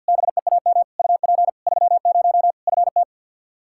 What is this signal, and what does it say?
Morse code, heard as a keyed tone, about 46 wpm, 700 Hz. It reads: BEAM UP 30 FT
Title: Practice QSO - Exchange 2 - Info - 50wpm